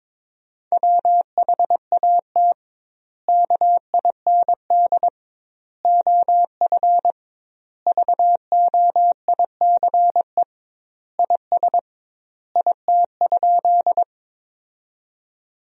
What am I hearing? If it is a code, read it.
Morse code, heard as a keyed tone, about 22 wpm, 700 Hz. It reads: WHAT KIND OF VOICE IS IT?
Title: 2x Word-Spacing - Sentences from Top 500 Words - 22wpm